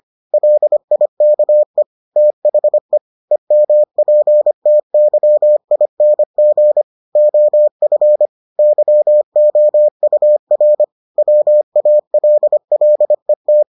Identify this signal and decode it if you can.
Morse code, heard as a keyed tone, about 25 wpm, 600 Hz. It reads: LIKE THE EMPTYING OF YOUR WALLET